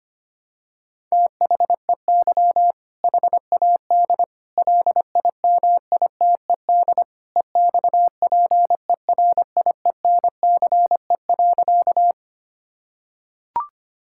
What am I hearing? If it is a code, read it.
Morse code, heard as a keyed tone, about 25 wpm, 700 Hz. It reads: THEY HAD LIMITED EXPERIENCE.
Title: Sentences from Top 1000 Words - Review Part 6 - 25wpm